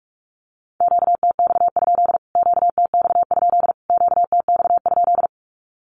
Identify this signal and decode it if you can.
Morse code, heard as a keyed tone, about 44 wpm, 700 Hz. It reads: QTX? QTX? QTX?
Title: Q-Codes Repeated 3x 45wpm